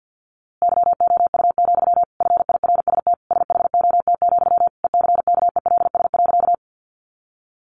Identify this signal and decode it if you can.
Morse code, heard as a keyed tone, about 50 wpm, 700 Hz. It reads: YOU, FIRST SHOT, ECKELS.
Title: A Sound Of Thunder 50wpm